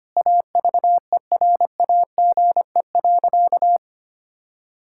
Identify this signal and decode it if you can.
Morse code, heard as a keyed tone, about 25 wpm, 700 Hz. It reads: AVERAGE.
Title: Sentences from Top 1000 Words - Review Part 6 - 25wpm